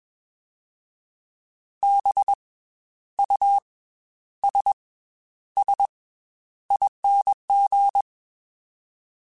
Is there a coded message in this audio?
Yes: Morse code from 1.8 to 8.0 s, heard as a keyed tone, about 21 wpm, spaced as for 11 wpm, 800 Hz. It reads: BUSSING